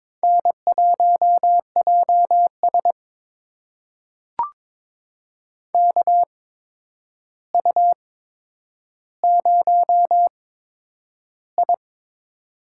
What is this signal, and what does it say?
Morse code, heard as a keyed tone, about 22 wpm, spaced as for 9 wpm, 700 Hz. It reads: N1JS KU0I
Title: USA Call Signs - 4 Characters - 8x Character Spacing - 22wpm